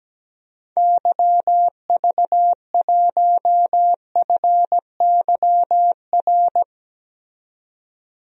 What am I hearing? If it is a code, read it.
Morse code, heard as a keyed tone, about 17 wpm, 700 Hz. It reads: YV1FYR
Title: DXCC-100 Call Signs - 6 Characters - 17wpm